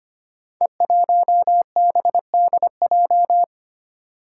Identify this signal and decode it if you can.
Morse code, heard as a keyed tone, about 25 wpm, 700 Hz. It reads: E1BDJ
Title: Code Groups - 5 Letter:Numbers - 25wpm